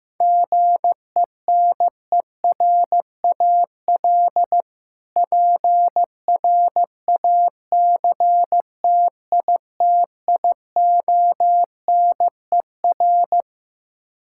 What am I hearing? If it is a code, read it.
Morse code, heard as a keyed tone, about 16 wpm, 700 Hz. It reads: GENERAL PRACTITIONER
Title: Practice QSO - Exchange 3 - More Info - 15wpm